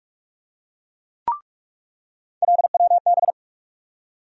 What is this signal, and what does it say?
Morse code, heard as a keyed tone, about 43 wpm, 700 Hz. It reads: LWB
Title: USA Call Sign Postfixes - 3 Letters 45wpm